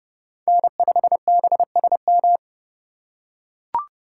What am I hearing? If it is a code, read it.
Morse code, heard as a keyed tone, about 28 wpm, 700 Hz. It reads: N5BSM